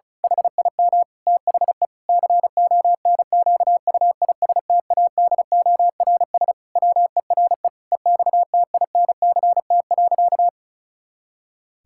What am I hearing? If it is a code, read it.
Morse code, heard as a keyed tone, about 34 wpm, 700 Hz. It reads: HIM THE CONQUISTADORS WERE EXTINCT.
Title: Pangram Challenge v2 - 35wpm